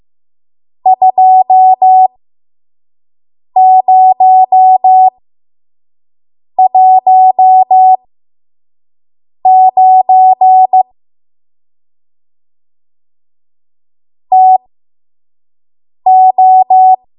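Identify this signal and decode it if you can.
Morse code, heard as a keyed tone, about 15 wpm, spaced as for 5 wpm, 750 Hz. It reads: 2019 TO
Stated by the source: 5 WPM Morse code practice.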